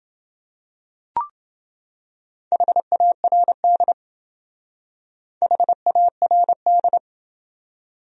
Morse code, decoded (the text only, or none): HARD HARD